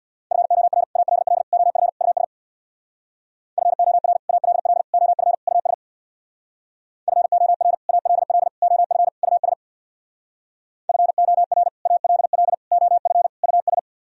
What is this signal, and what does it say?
Morse code, heard as a keyed tone, about 64 wpm, 700 Hz. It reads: FOR ALL OF US FOR ALL OF US FOR ALL OF US FOR ALL OF US